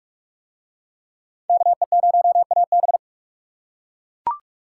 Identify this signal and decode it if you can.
Morse code, heard as a keyed tone, about 41 wpm, 700 Hz. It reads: KE0AB